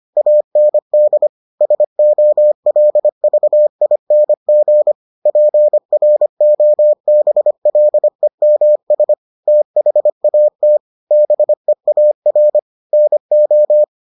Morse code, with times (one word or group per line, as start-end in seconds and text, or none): 0.2-1.3 AND
1.6-4.9 SOLVING
5.3-9.1 PROBLEMS
9.5-10.8 THAT
11.1-12.6 BEAR
12.9-13.8 NO